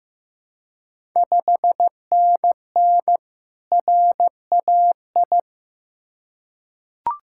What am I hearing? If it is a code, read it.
Morse code, heard as a keyed tone, about 15 wpm, 690 Hz. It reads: 5NN RAI
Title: Ontario QSO Party Exchange 15wpm